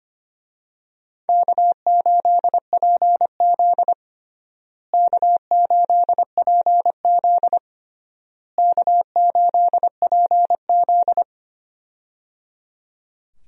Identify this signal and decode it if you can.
Morse code, heard as a keyed tone, about 25 wpm, 700 Hz. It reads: K8PZ K8PZ K8PZ
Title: USA Call Signs - 4 Characters - Repeated 3x - 25wpm